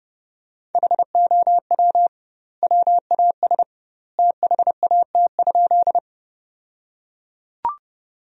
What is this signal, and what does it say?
Morse code, heard as a keyed tone, about 29 wpm, 700 Hz. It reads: HOW WAS THAT?
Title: 2x Word-Spacing - Sentences from Top 100 Words - 30wpm